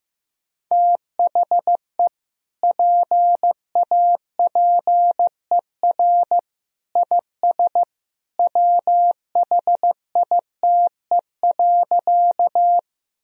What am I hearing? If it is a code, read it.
Morse code, heard as a keyed tone, about 16 wpm, 700 Hz. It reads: THE PAPER IS WHITE.
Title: Sentences from Top 1000 Words - Review Part 1 - 15wpm